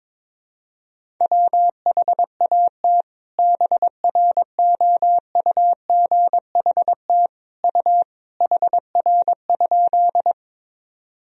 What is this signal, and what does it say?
Morse code, heard as a keyed tone, about 22 wpm, 700 Hz. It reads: WHAT BROUGHT U HR?
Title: Sentences from Top 500 Words and Abbreviations 22wpm